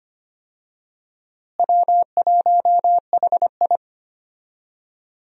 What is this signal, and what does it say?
Morse code, heard as a keyed tone, about 25 wpm, 700 Hz. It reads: W1HI
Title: USA Call Signs - 4 Characters - Repeated 3x - 25wpm